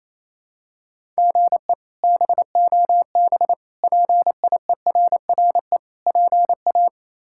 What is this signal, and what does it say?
Morse code, heard as a keyed tone, about 28 wpm, 700 Hz. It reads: GE BOB PIERRE PA
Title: K1USN SST QSO Practice - 28wpm